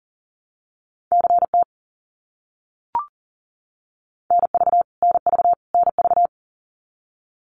Morse code, heard as a keyed tone, about 37 wpm, 700 Hz. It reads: CT NV NV NV
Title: US States Repeated 3x 40wpm